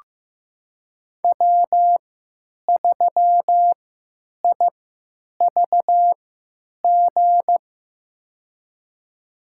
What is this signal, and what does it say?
Morse code, heard as a keyed tone, about 15 wpm, spaced as for 9 wpm, 700 Hz. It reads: W3IVG
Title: USA Call Signs - 5 Characters - 3x Character Spacing - 15wpm